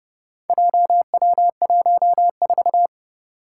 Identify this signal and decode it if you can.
Morse code, heard as a keyed tone, about 30 wpm, 700 Hz. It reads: JW14
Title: Four Challenging Letter-Number Combinations 30wpm